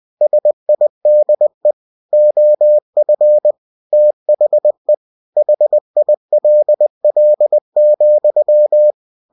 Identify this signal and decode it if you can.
Morse code, heard as a keyed tone, about 20 wpm, 600 Hz. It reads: SIDE OF THE HILL,
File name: cw-books / winnie-the-pooh / winnie-20wpm-page-0066